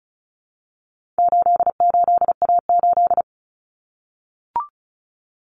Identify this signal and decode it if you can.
Morse code, heard as a keyed tone, about 34 wpm, 700 Hz. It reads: 88A8
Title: Common Cut Numbers - 4-Digit 35wpm